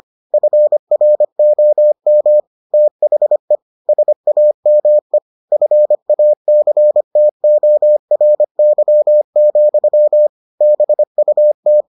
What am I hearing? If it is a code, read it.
Morse code, heard as a keyed tone, about 25 wpm, 600 Hz. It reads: FROM THE SAME FACTORY, BUT